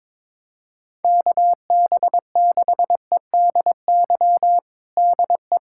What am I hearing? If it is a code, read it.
Morse code, heard as a keyed tone, about 22 wpm, 700 Hz. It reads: KB6EDY DE
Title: USA Callsign DE Callsign - 6 Characters - 22wpm